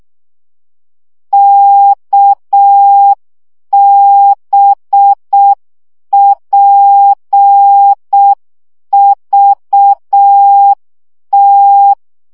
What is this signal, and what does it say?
Morse code, heard as a keyed tone, about 6 wpm, 800 Hz. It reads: KBPVT
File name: morze vizsga 2011-1 5